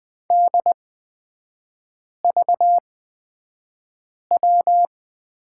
Morse code, heard as a keyed tone, about 20 wpm, spaced as for 5 wpm, 700 Hz. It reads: DVW
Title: 229 Lesson 37e - Letter x - Three Characters 20wpm